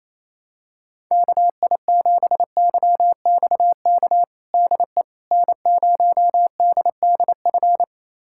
Morse code, heard as a keyed tone, about 28 wpm, 700 Hz. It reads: KI7YXK DE N0DDF
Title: USA Callsign DE Callsign 28wpm